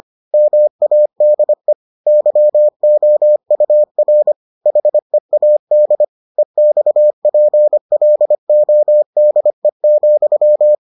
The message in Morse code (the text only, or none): MADE YOUR HEAD EXPLODE,